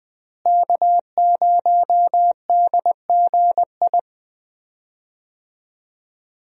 Morse code, heard as a keyed tone, about 20 wpm, 700 Hz. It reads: K0DGI
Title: USA Call Signs - 5 Characters - Repeated 2x - 20wpm